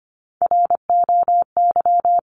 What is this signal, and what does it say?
Morse code, heard as a keyed tone, about 25 wpm, 700 Hz. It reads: ROY
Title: CWT Exchange 25wpm